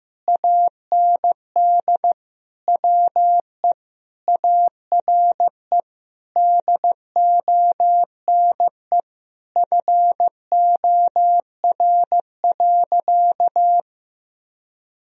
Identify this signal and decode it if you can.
Morse code, heard as a keyed tone, about 16 wpm, 700 Hz. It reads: AND WE ARE DONE FOR.
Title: Sentences from Top 1000 Words - Review Part 4 - 15wpm